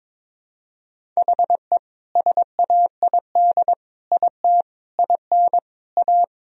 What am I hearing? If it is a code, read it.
Morse code, heard as a keyed tone, about 22 wpm, 700 Hz. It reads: HE SAID IT IN A
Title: Sentences from Top 1000 Words - Review Part 1 - 22wpm